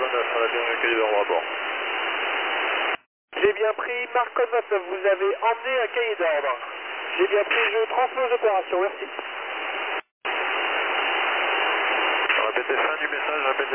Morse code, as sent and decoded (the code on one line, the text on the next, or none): none